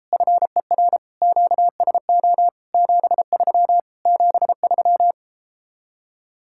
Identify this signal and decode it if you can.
Morse code, heard as a keyed tone, about 33 wpm, 700 Hz. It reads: FER QSO 73 73